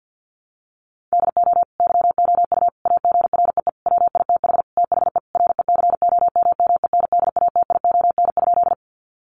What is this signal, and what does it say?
Morse code, heard as a keyed tone, about 50 wpm, 700 Hz. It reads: DO YOU AGREE WITH THE RECOMMENDATION?
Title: Sentences from Top 1400 Words 50wpm